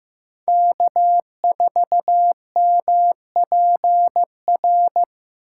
Morse code, heard as a keyed tone, about 15 wpm, 700 Hz. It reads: K4MPR
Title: USA Call Signs - 6x Character Spacing - 15wpm